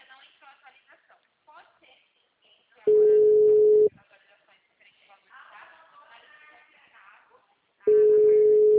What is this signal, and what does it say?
Ringback tone (Europe, 425 Hz), with speech.